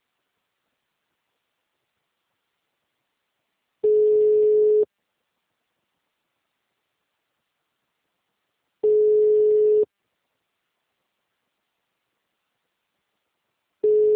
Ringback tone (Europe, 425 Hz).